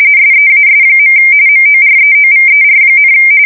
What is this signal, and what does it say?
RTTY, about 45.45 baud. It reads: SEAS. MOREOVER,
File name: fsk170 / battleshipnj